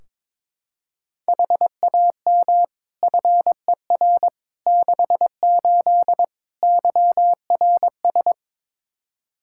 Morse code, heard as a keyed tone, about 22 wpm, 700 Hz. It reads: HAM FER 68 YRS
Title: K1USN SST QSO Practice 22wpm